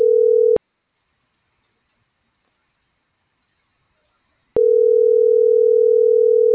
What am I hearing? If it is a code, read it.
Ringback tone (North America), 440 + 480 Hz.